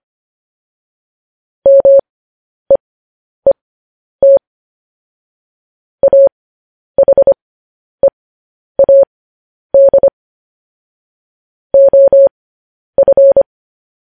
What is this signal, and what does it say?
Morse code, heard as a keyed tone, about 25 wpm, spaced as for 10 wpm, 550 Hz. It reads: MEET AHEAD OF
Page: news-headlines-in-morse-code-at-10-wpm